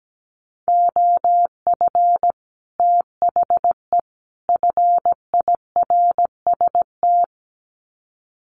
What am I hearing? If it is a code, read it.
Morse code, heard as a keyed tone, about 17 wpm, 700 Hz. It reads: OF THE FIRST
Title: Sets of 3 Words 17wpm